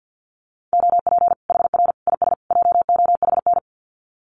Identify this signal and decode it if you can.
Morse code, heard as a keyed tone, about 50 wpm, 700 Hz. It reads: OP HR IS JOHN
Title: Common QSO Phrases 50wpm